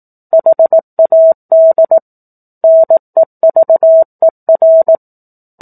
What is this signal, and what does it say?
Morse code, heard as a keyed tone, about 18 wpm, 650 Hz. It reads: HAD NEVER
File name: 04-Chapter-04 18 WPM